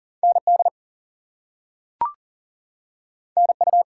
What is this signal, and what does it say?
Morse code, heard as a keyed tone, about 40 wpm, 700 Hz. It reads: ND NU
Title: US States and Canadian Provinces 40wpm